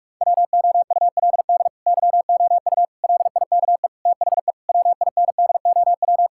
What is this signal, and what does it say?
Morse code, heard as a keyed tone, about 41 wpm, 700 Hz. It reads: WOULD YOU LIKE THE WINDOW
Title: Sentences from Top 1600 Words 45wpm